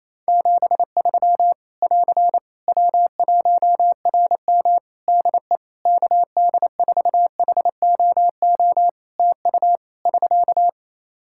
Morse code, heard as a keyed tone, about 28 wpm, 700 Hz. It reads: 73 <AR> W1RM DE KD4HOO TU <SK>
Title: Practice QSO - Exchange 4 - Ending - 28wpm